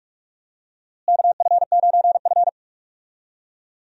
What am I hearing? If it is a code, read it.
Morse code, heard as a keyed tone, about 44 wpm, 700 Hz. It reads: KF9F
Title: USA Call Signs - 4 Characters - Repeated 2x - 45wpm